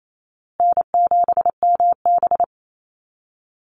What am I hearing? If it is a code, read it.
Morse code, heard as a keyed tone, about 28 wpm, 700 Hz. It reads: N7MB
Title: Easy Call Signs Repeated 3x - 28wpm